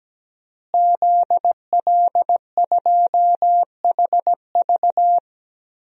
Morse code, heard as a keyed tone, about 17 wpm, 700 Hz. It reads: ZL2HV
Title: DXCC-100 Call Signs - 5 Characters - Repeated 3x - 17wpm